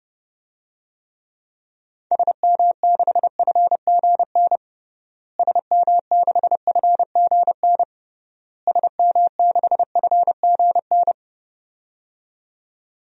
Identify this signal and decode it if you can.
Morse code, heard as a keyed tone, about 30 wpm, 700 Hz. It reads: SM6FGN SM6FGN SM6FGN
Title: DXCC-100 Call Signs - 6 Characters - Repeated 3x - 30wpm